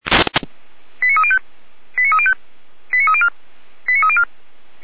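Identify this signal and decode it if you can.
Five-tone selective call (ZVEI), heard as quick runs of five tones, tones of about 70 ms. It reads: 89305 89305 89304 89304
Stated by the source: ZVEI five-tone call 89235 received by SDR on channel 455, from the fire brigade server feuerwehr-orlamuende.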